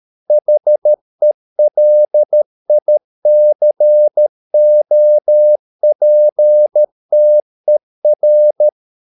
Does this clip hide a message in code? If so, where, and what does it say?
Yes: Morse code from 0.3 to 8.7 s, heard as a keyed tone, about 13 wpm, 600 Hz. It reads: HELICOPTER